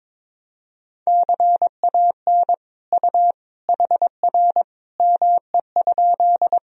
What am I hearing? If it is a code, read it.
Morse code, heard as a keyed tone, about 22 wpm, 700 Hz. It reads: CAN U HR ME?
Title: Sentences from Top 300 Words and Abbreviations 22wpm